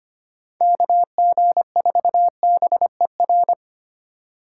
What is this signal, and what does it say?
Morse code, heard as a keyed tone, about 33 wpm, 700 Hz. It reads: KG4BER